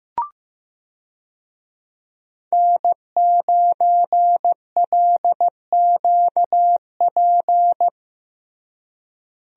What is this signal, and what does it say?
Morse code, heard as a keyed tone, about 15 wpm, 700 Hz. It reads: N9LQP
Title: USA Call Signs - 5 Characters - Repeated 3x - 15wpm